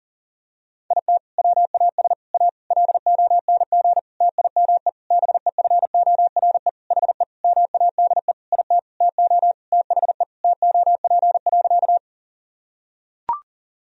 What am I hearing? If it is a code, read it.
Morse code, heard as a keyed tone, about 37 wpm, 700 Hz. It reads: IT WAS A LONG TIME BEFORE HE MADE IT TO THE TOP.
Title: Sentences from Top 1000 Words - Review Part 2 - 40wpm